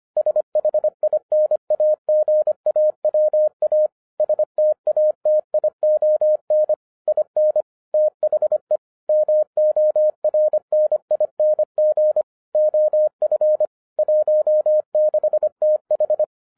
Morse code, heard as a keyed tone, about 25 wpm, 610 Hz. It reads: SHINAGAWA STATION IN THE MORNING OF 16TH
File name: 2011-07-mp3 / n44 JOHN K3TN 25wpm